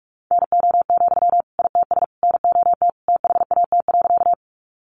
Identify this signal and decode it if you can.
Morse code, heard as a keyed tone, about 44 wpm, 700 Hz. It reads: NO, ITS NOT THAT.
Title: Sentences from Top 600 Words 45wpm